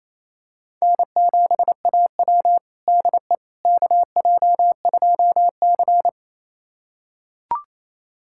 Morse code, heard as a keyed tone, about 28 wpm, 700 Hz. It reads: N7AW DE KJ2C
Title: USA Callsign DE Callsign - 4 Characters - 28wpm